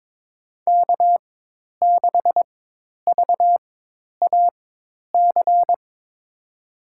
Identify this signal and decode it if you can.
Morse code, heard as a keyed tone, about 22 wpm, spaced as for 10 wpm, 700 Hz. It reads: K6VAC
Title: USA Call Signs - 5 Characters - 4x Character Spacing - 22wpm